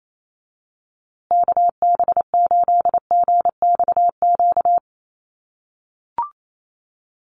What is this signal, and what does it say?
Morse code, heard as a keyed tone, about 28 wpm, 700 Hz. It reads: KB8GXQ
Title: USA Call Signs Repeated 2x 28wpm